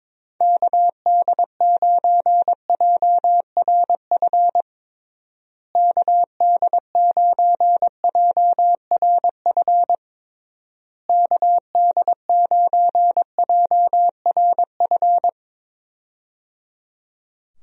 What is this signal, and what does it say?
Morse code, heard as a keyed tone, about 22 wpm, 700 Hz. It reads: KD9JRF KD9JRF KD9JRF
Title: USA Call Signs - 6 Characters - Repeated 3x - 22wpm